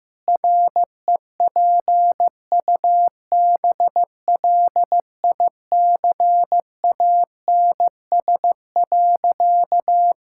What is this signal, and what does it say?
Morse code, heard as a keyed tone, about 15 wpm, 700 Hz. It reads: REPUBLICANS.
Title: Pangram Challenge v2 - 15wpm